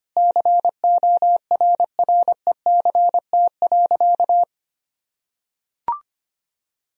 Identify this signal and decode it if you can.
Morse code, heard as a keyed tone, about 25 wpm, 700 Hz. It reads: CORRECT.
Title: Sentences from Top 1000 Words - Review Part 2 - 25wpm